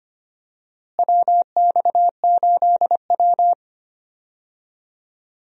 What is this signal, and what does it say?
Morse code, heard as a keyed tone, about 25 wpm, 700 Hz. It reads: WX8W
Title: USA Call Signs - 4 Characters - Repeated 3x - 25wpm